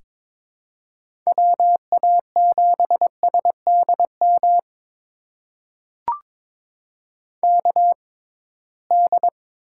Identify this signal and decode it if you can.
Morse code, heard as a keyed tone, about 22 wpm, spaced as for 17 wpm, 700 Hz. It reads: WA7SDM KD
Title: USA Call Signs - 6 Characters - 6x Character Spacing - 22wpm